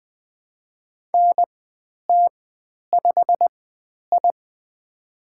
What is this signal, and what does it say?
Morse code, heard as a keyed tone, about 20 wpm, spaced as for 10 wpm, 700 Hz. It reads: NT5I
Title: 095 Lesson 16g - Letter c - Call Signs 20wpm